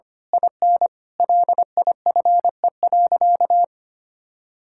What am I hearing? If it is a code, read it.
Morse code, heard as a keyed tone, about 25 wpm, 700 Hz. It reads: IN LIFE.
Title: Sentences from Top 1000 Words - Review Part 6 - 25wpm